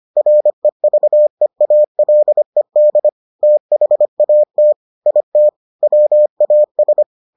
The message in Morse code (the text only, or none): REVEALED THAT IT WAS